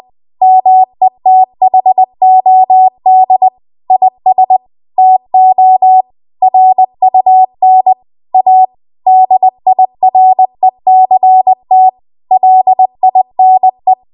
Morse code, heard as a keyed tone, about 20 wpm, 750 Hz. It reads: METHOD IS TO RUN A DIRECT LINE
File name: Archive / 20 WPM / 20WPM